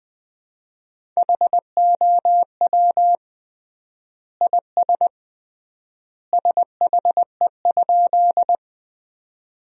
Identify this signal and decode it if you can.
Morse code, heard as a keyed tone, about 21 wpm, 700 Hz. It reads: HOW IS SHE?
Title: 3x Word-Spacing - Sentences from Top 100 Words - 20wpm